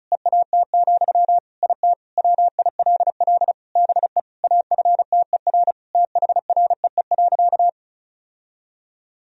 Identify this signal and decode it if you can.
Morse code, heard as a keyed tone, about 34 wpm, 700 Hz. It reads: EAT, IT WILL BE AFTER THREE.
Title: Sentences from Top 1000 Words - Review Part 1 - 35wpm